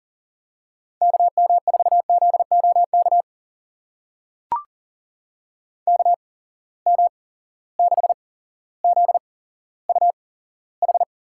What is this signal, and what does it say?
Morse code, heard as a keyed tone, about 40 wpm, spaced as for 17 wpm, 700 Hz. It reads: KM4ZOK KM6ZUH
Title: USA Call Signs - 6 Characters - 8x Character Spacing - 40wpm